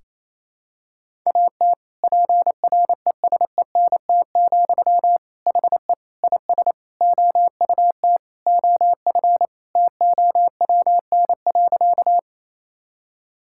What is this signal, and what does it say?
Morse code, heard as a keyed tone, about 28 wpm, 700 Hz. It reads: AT PRESENT, HE IS OUT OF TOWN.
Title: Sentences from Top 1000 Words - Review Part 5 - 28wpm